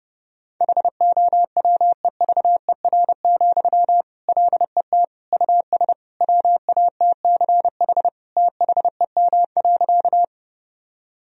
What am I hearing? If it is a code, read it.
Morse code, heard as a keyed tone, about 30 wpm, 700 Hz. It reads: HOWEVER, LET US WATCH THEM.
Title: Sentences from Top 1000 Words - Review Part 1 - 30wpm